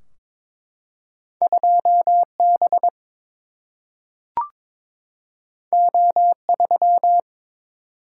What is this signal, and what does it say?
Morse code, heard as a keyed tone, about 22 wpm, 700 Hz. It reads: 2 B O 3
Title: Single-Letter Single-Number 22wpm